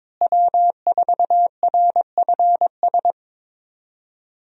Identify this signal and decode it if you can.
Morse code, heard as a keyed tone, about 22 wpm, 700 Hz. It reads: W4RFS
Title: USA Call Signs - 5 Characters - Repeated 2x - 22wpm